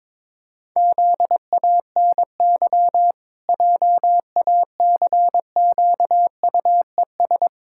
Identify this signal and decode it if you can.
Morse code, heard as a keyed tone, about 22 wpm, 700 Hz. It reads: ZANY JACQUES
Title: Pangram Challenge v2 - 22wpm